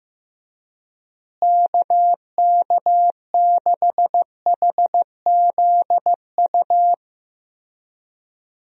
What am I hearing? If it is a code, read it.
Morse code, heard as a keyed tone, about 15 wpm, 700 Hz. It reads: KK6HZU
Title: USA Call Signs - 6 Characters - Repeated 3x - 15wpm